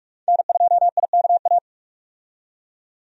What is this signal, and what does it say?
Morse code, heard as a keyed tone, about 44 wpm, 700 Hz. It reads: N2IKA
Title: USA Call Signs - 5 Characters - 45wpm